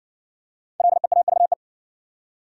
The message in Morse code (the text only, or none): LEAVE